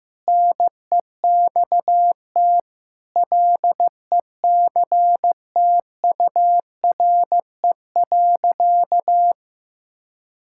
Morse code, heard as a keyed tone, about 15 wpm, 700 Hz. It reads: NEXT LECTURE.
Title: Sentences from Top 1600 Words 15wpm